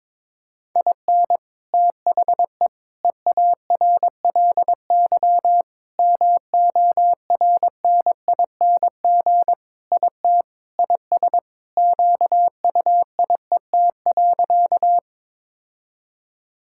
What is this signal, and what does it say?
Morse code, heard as a keyed tone, about 22 wpm, 700 Hz. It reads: IN THE EARLY MORNING IT IS QUIET.
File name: practice / 22wpm / Sentences with 3-Word Phrases - 22wpm